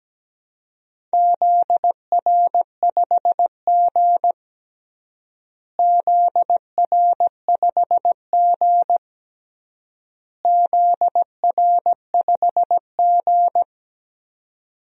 Morse code, heard as a keyed tone, about 17 wpm, 700 Hz. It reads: ZR5G ZR5G ZR5G
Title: DXCC-100 Call Signs - 4 Characters - Repeated 3x - 17wpm